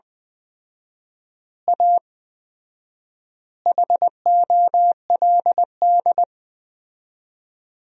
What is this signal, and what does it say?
Morse code, heard as a keyed tone, about 20 wpm, 700 Hz. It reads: A HOLD
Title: 4x Word-Spacing - Sentences from Top 400 Words - 20wpm